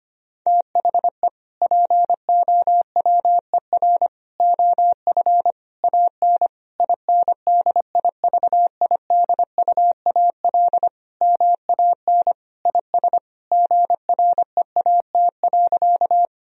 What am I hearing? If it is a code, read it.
Morse code, heard as a keyed tone, about 25 wpm, 700 Hz. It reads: THE POWER OF AN INDIVIDUAL MAN IS GREAT.
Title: Sentences from Top 1000 Words - Review Part 4 - 25wpm